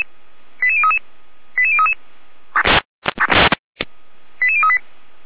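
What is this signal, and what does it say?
Five-tone selective call (ZVEI), heard as quick runs of five tones, tones of about 70 ms. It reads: 88033 88033 88038
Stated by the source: Five-tone ZVEI selective call received on an SDR.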